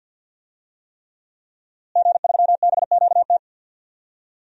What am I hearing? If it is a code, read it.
Morse code, heard as a keyed tone, about 47 wpm, 700 Hz. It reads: G3BQT
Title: DXCC-100 Call Signs - 5 Characters - Repeated 3x - 50wpm